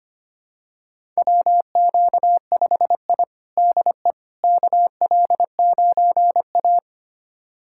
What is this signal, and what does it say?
Morse code, heard as a keyed tone, about 25 wpm, 700 Hz. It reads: WQ5I DE KL9A
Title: USA Callsign DE Callsign - 4 Characters - 25wpm